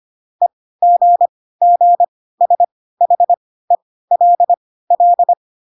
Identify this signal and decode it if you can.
Morse code, heard as a keyed tone, about 25 wpm, spaced as for 16 wpm, 700 Hz. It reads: EGGSHELL